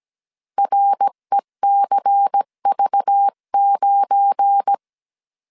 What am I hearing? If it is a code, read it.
Morse code, heard as a keyed tone, about 17 wpm, 800 Hz. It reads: RECV9
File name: CW / DET / 1 9 1500C S B 16wpm 2856